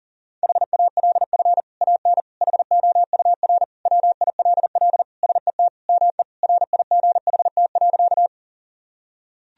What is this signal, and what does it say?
Morse code, heard as a keyed tone, about 40 wpm, 700 Hz. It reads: HALF AN HOUR WILL SET ME RIGHT.
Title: Sentences from Top 1000 Words - Review Part 1 - 40wpm